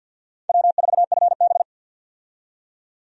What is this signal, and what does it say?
Morse code, heard as a keyed tone, about 52 wpm, 700 Hz. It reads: W4FB
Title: USA Call Signs - 4 Characters - 50wpm